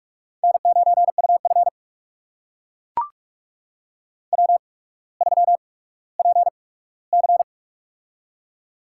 Morse code, heard as a keyed tone, about 44 wpm, spaced as for 19 wpm, 700 Hz. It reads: N9UF W3PC
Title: USA Call Signs - 8x Character Spacing - 45wpm